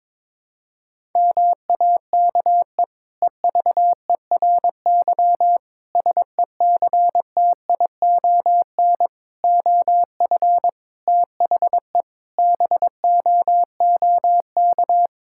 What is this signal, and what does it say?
Morse code, heard as a keyed tone, about 22 wpm, 700 Hz. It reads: MAKE EVERY SECTION OF THE BOOK
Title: Sentences from Top 1000 Words - Review Part 5 - 22wpm